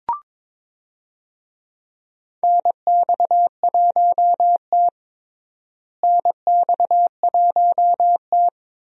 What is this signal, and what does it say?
Morse code, heard as a keyed tone, about 22 wpm, 700 Hz. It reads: NX1T NX1T
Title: USA Call Signs - 4 Characters - Repeated 3x - 22wpm